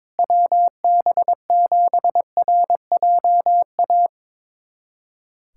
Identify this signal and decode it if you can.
Morse code, heard as a keyed tone, about 22 wpm, 700 Hz. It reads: WB7RJA
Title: USA Callsign DE Callsign 22wpm